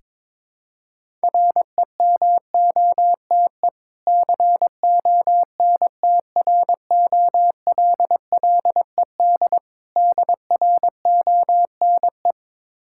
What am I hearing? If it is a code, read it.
Morse code, heard as a keyed tone, about 21 wpm, 690 Hz. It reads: REMOTE CONTROLLED DRONE
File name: Difficult Words with Hints - 22wpm